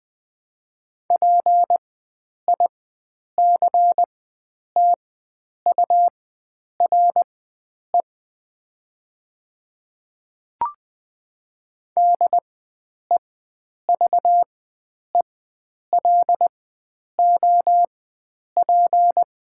Morse code, heard as a keyed tone, about 20 wpm, spaced as for 9 wpm, 700 Hz. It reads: PICTURE DEVELOP
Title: 4x Character Spacing - Top 500 Words - 20wpm